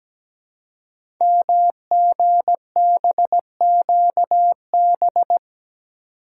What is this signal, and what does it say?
Morse code, heard as a keyed tone, about 17 wpm, 700 Hz. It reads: MGBQB
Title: Code Groups - 5 Letters - 17wpm